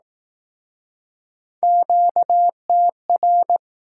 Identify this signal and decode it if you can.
Morse code, heard as a keyed tone, about 20 wpm, 700 Hz. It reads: QTR